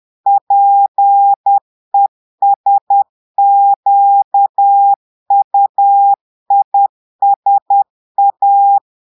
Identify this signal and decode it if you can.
Morse code, heard as a keyed tone, about 10 wpm, 800 Hz. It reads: PESQUISA
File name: Portugais - texte en clair 1 - vitesse - 10 mots minute